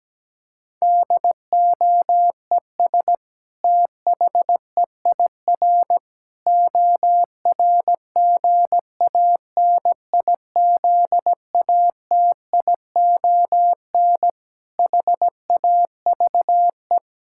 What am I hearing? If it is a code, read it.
Morse code, heard as a keyed tone, about 17 wpm, 700 Hz. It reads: DOES THEIR ORGANIZATION HAVE
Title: Sentences from Top 1000 Words - Review Part 5 - 17wpm